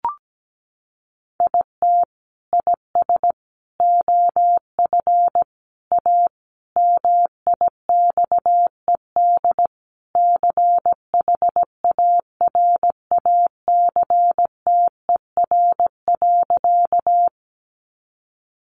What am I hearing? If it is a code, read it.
Morse code, heard as a keyed tone, about 17 wpm, 700 Hz. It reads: IT IS OF A MIXED CHARACTER.
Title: Sentences from Top 1500 Words 17wpm